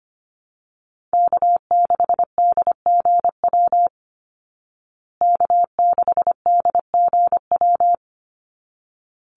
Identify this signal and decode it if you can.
Morse code, heard as a keyed tone, about 25 wpm, 700 Hz. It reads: K6DGW K6DGW
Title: Call Signs Repeated 3x - 25wpm